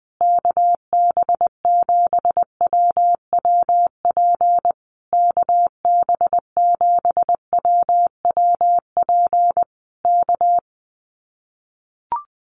Morse code, heard as a keyed tone, about 20 wpm, 700 Hz. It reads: KB7WWP KB7WWP K
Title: Common QSO Phrases Encore 20wpm